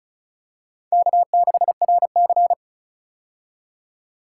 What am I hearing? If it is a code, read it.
Morse code, heard as a keyed tone, about 34 wpm, 700 Hz. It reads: K6RC
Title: K1USN SST QSO Practice - 35wpm